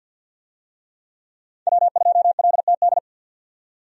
Morse code, heard as a keyed tone, about 52 wpm, 700 Hz. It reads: W2LTD
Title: USA Call Signs - 5 Characters - 50wpm